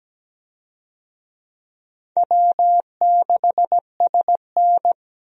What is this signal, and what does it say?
Morse code, heard as a keyed tone, about 17 wpm, 700 Hz. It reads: W6SN
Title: USA Call Signs - 4 Characters - 17wpm